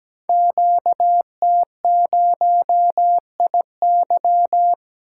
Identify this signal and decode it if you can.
Morse code, heard as a keyed tone, about 17 wpm, 700 Hz. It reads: QT0IY
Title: Code Groups - 5 Letter:Numbers - 17wpm